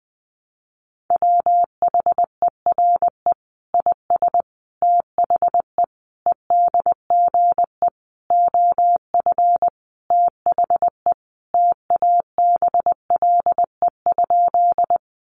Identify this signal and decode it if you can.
Morse code, heard as a keyed tone, about 20 wpm, 700 Hz. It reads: WHERE IS THE EDGE OF THE TABLE?
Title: Sentences from Top 1300 Words 20wpm